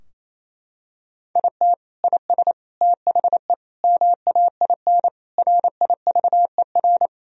Morse code, heard as a keyed tone, about 27 wpm, 700 Hz. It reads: IT IS THE MAIN RIVER
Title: Sentences from Top 1000 Words - Review Part 2 - 28wpm